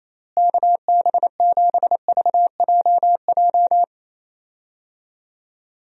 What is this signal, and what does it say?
Morse code, heard as a keyed tone, about 28 wpm, 700 Hz. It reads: KB7VJJ
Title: USA Call Signs - 6 Characters - Repeated 3x - 28wpm